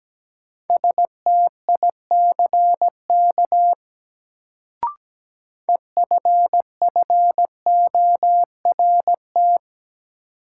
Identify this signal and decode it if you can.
Morse code, heard as a keyed tone, about 17 wpm, 700 Hz. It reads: STICK EFFORT
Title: Sentences from Top 1200 Words - New Word Drill - 17wpm